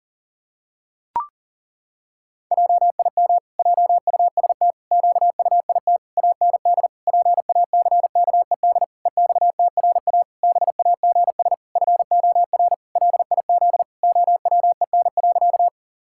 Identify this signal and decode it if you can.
Morse code, heard as a keyed tone, about 40 wpm, 700 Hz. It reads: JIM JUST QUIT AND PACKED EXTRA BAGS FOR LIZ OWEN.
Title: Pangram Challenge v2 - 40wpm